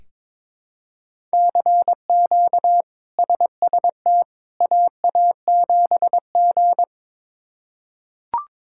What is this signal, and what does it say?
Morse code, heard as a keyed tone, about 21 wpm, 700 Hz. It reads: CQ SST AA7G
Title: K1USN SST QSO Practice - 22wpm